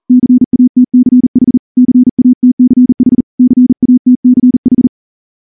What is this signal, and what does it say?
Morse code, heard as a keyed tone, about 38 wpm, 250 Hz. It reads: CATCH CATCH CATCH
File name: morse-training-40WPM